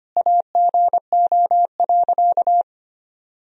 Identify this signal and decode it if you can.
Morse code, heard as a keyed tone, about 25 wpm, 700 Hz. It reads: AGO.
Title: Sentences from Top 500 Words and Abbreviations 25wpm